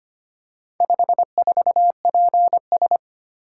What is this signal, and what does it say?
Morse code, heard as a keyed tone, about 25 wpm, 700 Hz. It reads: 54PS
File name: practice / 25wpm / Four Letter-Number Combinations - 25wpm